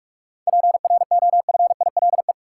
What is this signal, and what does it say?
Morse code, heard as a keyed tone, about 46 wpm, 700 Hz. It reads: PROFILE
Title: Seven Letter Words - 45wpm